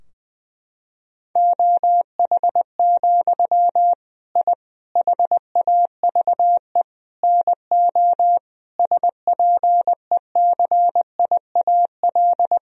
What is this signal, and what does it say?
Morse code, heard as a keyed tone, about 20 wpm, 700 Hz. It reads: OH, I HAVE NO SPECIAL
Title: Sentences from Top 1000 Words - Review Part 3 - 20wpm